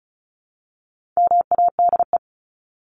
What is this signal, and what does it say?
Morse code, heard as a keyed tone, about 34 wpm, 700 Hz. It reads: MADE
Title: Top 200 Words 35wpm